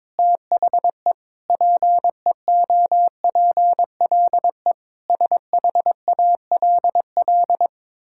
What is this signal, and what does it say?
Morse code, heard as a keyed tone, about 22 wpm, 700 Hz. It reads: THE PEOPLE SHALL
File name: Sentences with 9-Word Phrases - 22wpm